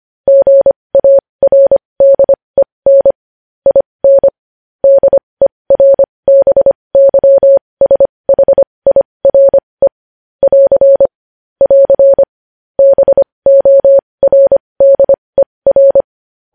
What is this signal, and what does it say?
Morse code, heard as a keyed tone, about 25 wpm, spaced as for 20 wpm, 550 Hz. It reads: GARDEN IN DERBYSHIRE <AR> <AR> BORDER
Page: news headlines in morse code at 20 wpm